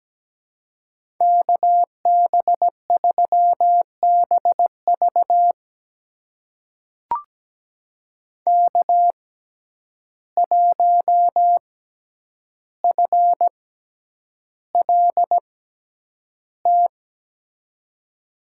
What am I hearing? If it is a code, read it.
Morse code, heard as a keyed tone, about 17 wpm, spaced as for 9 wpm, 700 Hz. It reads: KB3BV K1FLT
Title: USA Call Signs - 5 Characters - 6x Character Spacing - 17wpm